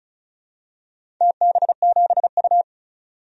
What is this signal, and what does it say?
Morse code, heard as a keyed tone, about 34 wpm, 700 Hz. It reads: TB7U
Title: DXCC-100 Call Signs - 4 Characters - 35wpm